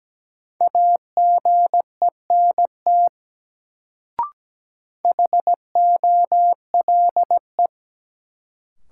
Morse code, heard as a keyed tone, about 17 wpm, 700 Hz. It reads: AGENT HOLE
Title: Sentences from Top 1600 Words - New Word Drill Alt - 17wpm